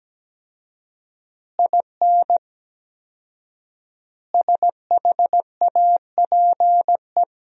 Morse code, heard as a keyed tone, about 17 wpm, 700 Hz. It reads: IN SHAPE
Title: 4x Word-Spacing - Sentences from Top 500 Words - 17wpm